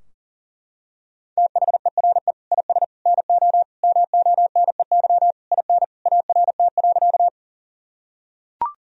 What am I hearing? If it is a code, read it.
Morse code, heard as a keyed tone, about 40 wpm, 700 Hz. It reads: THERE IS NO MONEY IN ART.
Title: Sentences from Top 1000 Words - Review Part 3 - 40wpm